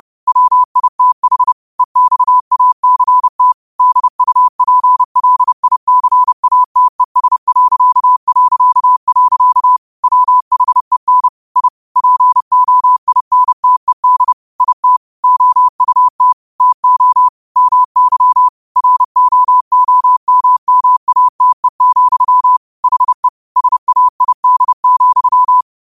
Morse code, heard as a keyed tone, about 30 wpm, 1000 Hz. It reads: WITH EXACT DUPLICATES... WHEN I POINTED IT OUT TO MY ROOMMATE, HE SAID,
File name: qotdfiles / 30wpm